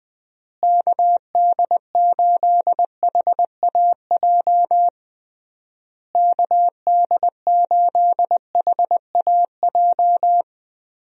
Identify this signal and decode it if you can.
Morse code, heard as a keyed tone, about 21 wpm, 700 Hz. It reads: KD8HAJ KD8HAJ
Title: USA Call Signs - 6 Characters - Repeated 3x - 20wpm